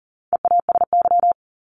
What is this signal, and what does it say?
Morse code, heard as a keyed tone, about 40 wpm, 700 Hz. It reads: EASY
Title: Top 501-750 Words 40wpm